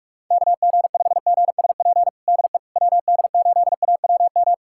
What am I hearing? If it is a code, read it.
Morse code, heard as a keyed tone, about 46 wpm, 700 Hz. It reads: KG5GSP DE WD8AWM
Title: USA Callsign DE Callsign 45wpm